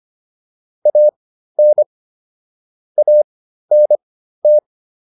Morse code, heard as a keyed tone, about 25 wpm, spaced as for 13 wpm, 600 Hz. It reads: AN ANT